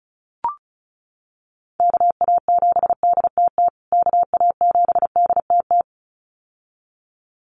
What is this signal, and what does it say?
Morse code, heard as a keyed tone, about 34 wpm, 700 Hz. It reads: KA7DTT KA7DTT
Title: USA Call Signs Repeated 2x 35wpm